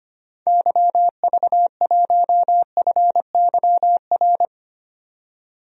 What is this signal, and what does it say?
Morse code, heard as a keyed tone, about 25 wpm, 700 Hz. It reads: YV1FYR
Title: DXCC-100 Call Signs - 6 Characters - 25wpm